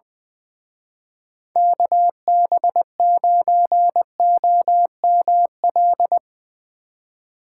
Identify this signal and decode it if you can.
Morse code, heard as a keyed tone, about 21 wpm, 700 Hz. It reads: KB9OML
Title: Speed Racing - USA 2x3 Call Signs 20wpm